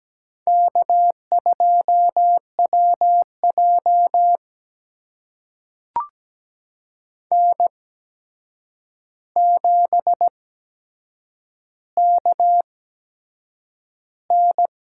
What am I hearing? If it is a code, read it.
Morse code, heard as a keyed tone, about 17 wpm, spaced as for 7 wpm, 700 Hz. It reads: K2WJ N7KN
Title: USA Call Signs - 4 Characters - 8x Character Spacing - 17wpm